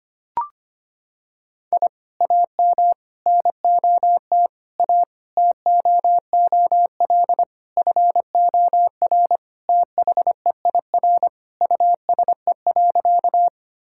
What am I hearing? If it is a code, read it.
Morse code, heard as a keyed tone, about 25 wpm, 700 Hz. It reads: I AM NOT A TOOL FOR THEIR USE.
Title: Sentences from Top 1000 Words - Review Part 6 - 25wpm